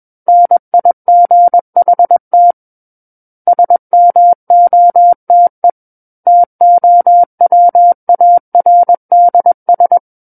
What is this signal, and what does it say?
Morse code, heard as a keyed tone, about 21 wpm, 700 Hz. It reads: NIGHT SMOTE TOWARDS
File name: Book One / 12-Chapter 12 - 21 WPM